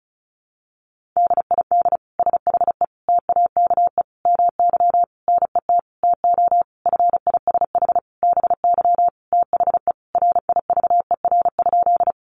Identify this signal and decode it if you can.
Morse code, heard as a keyed tone, about 34 wpm, 700 Hz. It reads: DID SHE TAKE MY NET TO FISH BY THE RIVER?
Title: Sentences from Top 1400 Words 35wpm